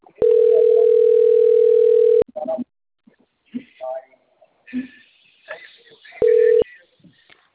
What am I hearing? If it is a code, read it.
Ringback tone (North America), 440 + 480 Hz, with speech.
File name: out- - - - - .30111